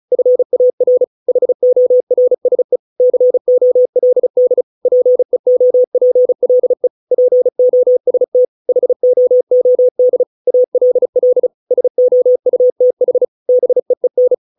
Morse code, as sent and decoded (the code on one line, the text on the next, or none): ..-. .- .-. / .... --- .-. ... / -.-. --- .-.. -.. / .--. --- .--. .-.. / .--. --- ... - / .... --- --- -.. / .- .-.. .-.. / ... --- ..- - .... / -... . -.
FAR HORSE COLD PEOPLE POST HOOD ALL SOUTH BEEN